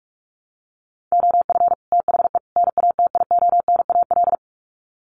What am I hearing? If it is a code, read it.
Morse code, heard as a keyed tone, about 44 wpm, 700 Hz. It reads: OF THE NATIONAL
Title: Sets of 3 Words 45wpm